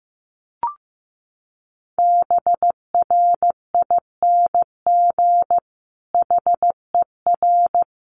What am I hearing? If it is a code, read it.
Morse code, heard as a keyed tone, about 15 wpm, 700 Hz. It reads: BRING HER
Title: Sentences from Top 1000 Words 15wpm